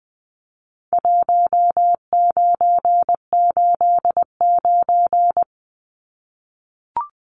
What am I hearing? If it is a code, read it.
Morse code, heard as a keyed tone, about 20 wpm, 700 Hz. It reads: 1989
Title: Numbers - 4-Digit 20wpm